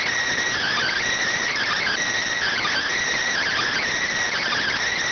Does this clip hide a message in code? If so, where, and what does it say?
Yes: Emergency Alert System (SAME) tones (header bursts) from 0.6 to 4.8 s, heard as harsh buzzing bursts.